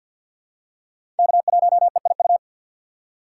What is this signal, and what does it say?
Morse code, heard as a keyed tone, about 52 wpm, 700 Hz. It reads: K1EIU
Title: USA Call Signs - 5 Characters - 3x Character Spacing - 50wpm